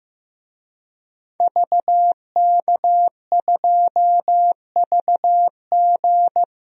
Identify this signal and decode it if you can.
Morse code, heard as a keyed tone, about 16 wpm, 700 Hz. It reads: VK2VG
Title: DXCC-100 Call Signs - 5 Characters - 15wpm